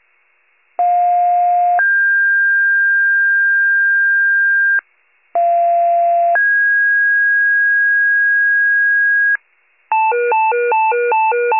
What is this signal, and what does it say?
Two-tone page, heard as a one-second tone, then a three-second tone, about 710 then 1600 Hz, about 690 then 1650 Hz. It is heard from 0.8 to 9.4 s.